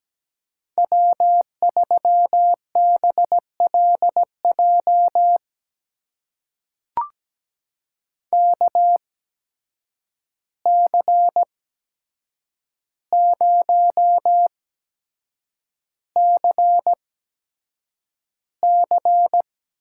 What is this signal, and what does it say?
Morse code, heard as a keyed tone, about 17 wpm, spaced as for 7 wpm, 700 Hz. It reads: W3BLJ KC0CC
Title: USA Call Signs - 5 Characters - 8x Character Spacing - 17wpm